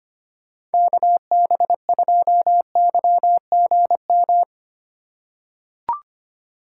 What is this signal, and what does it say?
Morse code, heard as a keyed tone, about 25 wpm, 700 Hz. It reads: KB2YGM
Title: USA Call Signs - 3x Character Spacing - 25wpm